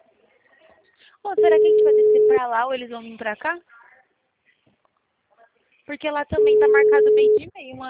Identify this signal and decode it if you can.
Ringback tone (Europe, 425 Hz), with speech.